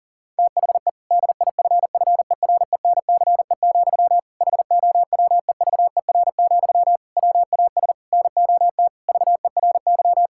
Morse code, heard as a keyed tone, about 40 wpm, 700 Hz. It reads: THE DIFFERENCE, HOWEVER, WAS NOT VERY
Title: Sentences from Top 1000 Words - Review Part 5 - 40wpm